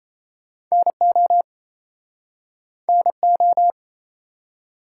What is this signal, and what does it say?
Morse code, heard as a keyed tone, about 39 wpm, 700 Hz. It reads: NO NO